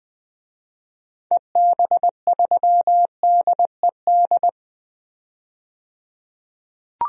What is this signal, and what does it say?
Morse code, heard as a keyed tone, about 20 wpm, 700 Hz. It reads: EB3DED
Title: DXCC-100 Call Signs - 6 Characters - 20wpm